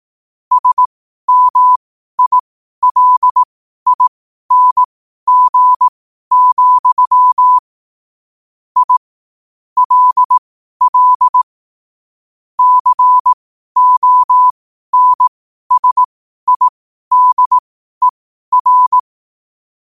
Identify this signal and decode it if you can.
Morse code, heard as a keyed tone, about 18 wpm, spaced as for 12 wpm, 1000 Hz. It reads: SMILING, I LL CONSIDER
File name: qotdfiles / 12wpm